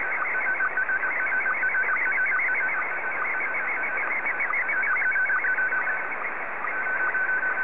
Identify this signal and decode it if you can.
SSTV (slow-scan television), Scottie 1.